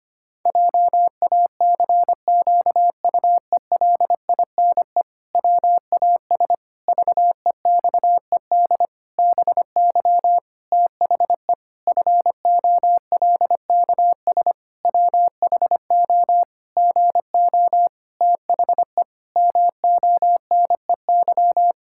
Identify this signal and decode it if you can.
Morse code, heard as a keyed tone, about 25 wpm, 700 Hz. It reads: JACQUELINE WAS VEXED BY THE FOLKS WHO GO THE MONEY
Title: Pangram Challenge v2 - 25wpm